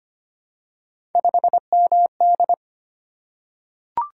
Morse code, heard as a keyed tone, about 25 wpm, 700 Hz. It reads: 5MD
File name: Three Letter-Number Combinations - 25wpm